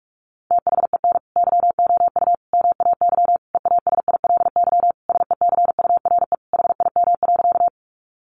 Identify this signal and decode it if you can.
Morse code, heard as a keyed tone, about 44 wpm, 700 Hz. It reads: THEN YOU MAY EASILY SECURE HIM.
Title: Sentences from Top 1200 Words 45wpm